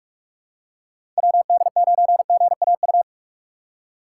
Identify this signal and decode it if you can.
Morse code, heard as a keyed tone, about 47 wpm, 700 Hz. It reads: WD9GAU
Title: USA Call Signs - 6 Characters - 3x Character Spacing - 45wpm